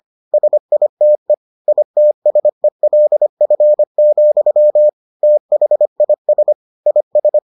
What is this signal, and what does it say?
Morse code, heard as a keyed tone, about 25 wpm, 600 Hz. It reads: SITE ITSELF, THIS IS